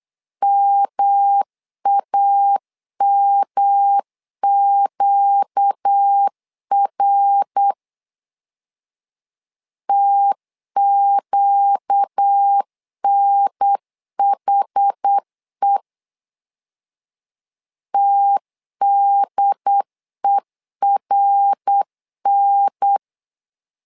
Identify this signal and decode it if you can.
Morse code, heard as a keyed tone, about 8 wpm, 800 Hz. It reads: MAMQR TQNHE TDERN
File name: 4 Q 8wpm